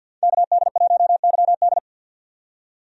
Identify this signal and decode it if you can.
Morse code, heard as a keyed tone, about 52 wpm, 700 Hz. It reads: KD1YD
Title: USA Call Signs - 5 Characters - 50wpm